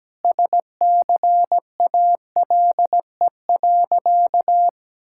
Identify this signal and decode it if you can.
Morse code, heard as a keyed tone, about 17 wpm, 700 Hz. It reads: SCALE.
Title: Sentences from Top 1000 Words - Review Part 5 - 17wpm